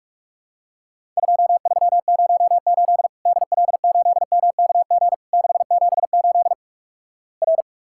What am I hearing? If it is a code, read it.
Morse code, heard as a keyed tone, about 48 wpm, 700 Hz. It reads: 2308 DL8MKG 678 R
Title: Worked All Europe DX Contest - QTC Exchange - 45wpm